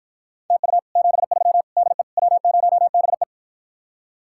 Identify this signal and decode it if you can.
Morse code, heard as a keyed tone, about 52 wpm, 700 Hz. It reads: TU 73 DE W0BE